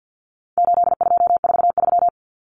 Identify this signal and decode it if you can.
Morse code, heard as a keyed tone, about 50 wpm, 700 Hz. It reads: 8243
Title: Common Cut Numbers - 4-Digit 50wpm